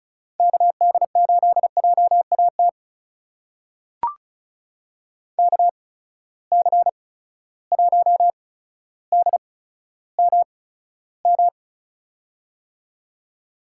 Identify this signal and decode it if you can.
Morse code, heard as a keyed tone, about 36 wpm, spaced as for 15 wpm, 700 Hz. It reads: KD8JAT KC1DMM
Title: USA Call Signs - 6 Characters - 8x Character Spacing - 35wpm